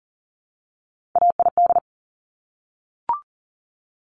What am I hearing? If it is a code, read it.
Morse code, heard as a keyed tone, about 42 wpm, 700 Hz. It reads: AID
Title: Top Three Letter Combinations in Words 40wpm